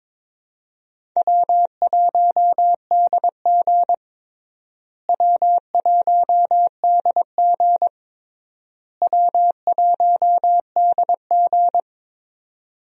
Morse code, heard as a keyed tone, about 22 wpm, 700 Hz. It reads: W1DG W1DG W1DG
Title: USA Call Signs - 4 Characters - Repeated 3x - 22wpm